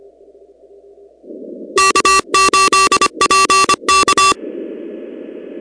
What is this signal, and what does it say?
Morse code, heard as a keyed tone, about 24 wpm, 1160 Hz, with noise. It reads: K8PK
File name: contests / 2019-cw-wpx-kt5j-k5pi- -2330 0-16